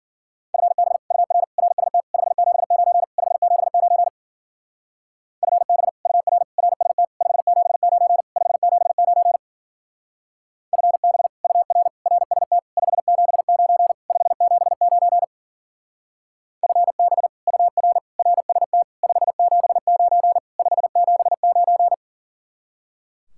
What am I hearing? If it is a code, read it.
Morse code, heard as a keyed tone, about 47 wpm, 700 Hz. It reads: FB UR RST 579 579 FB UR RST 579 579 FB UR RST 579 579 FB UR RST 579 579
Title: Speed-Racing Common QSO Phrases - 40wpm